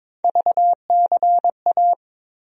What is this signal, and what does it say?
Morse code, heard as a keyed tone, about 22 wpm, 700 Hz. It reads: VCA
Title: USA Call Sign Postfixes - 3 Letters 22wpm